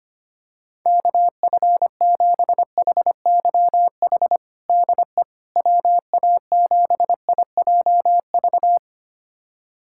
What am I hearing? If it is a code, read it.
Morse code, heard as a keyed tone, about 25 wpm, 700 Hz. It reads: KF7HYH DE WA7IJV
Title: USA Callsign DE Callsign - 6 Characters - 25wpm